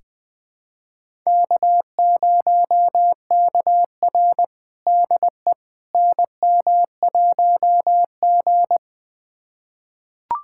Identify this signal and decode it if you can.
Morse code, heard as a keyed tone, about 20 wpm, 700 Hz. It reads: K0KR DE NM1G
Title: USA Callsign DE Callsign - 4 Characters - 20wpm